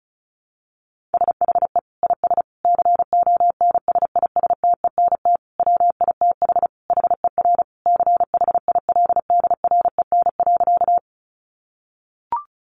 Morse code, heard as a keyed tone, about 34 wpm, 700 Hz. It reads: SHE IS CONSISTENT WITH HER CHILDREN.
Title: Sentences from Top 1500 Words 35wpm